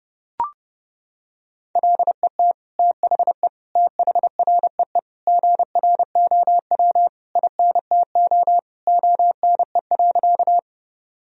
Morse code, heard as a keyed tone, about 30 wpm, 700 Hz. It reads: LET THE THREE GROW INTO ONE.
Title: Sentences from Top 1000 Words - Review Part 1 - 30wpm